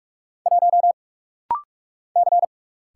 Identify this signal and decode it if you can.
Morse code, heard as a keyed tone, about 44 wpm, 700 Hz. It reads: 1 C
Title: Single Letter-Number Alt - Rapid-Fire 45wpm